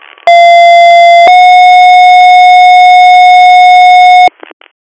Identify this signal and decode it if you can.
Two-tone page, heard as a one-second tone, then a three-second tone, about 690 then 730 Hz, with noise. It is heard from 0.3 to 4.3 s.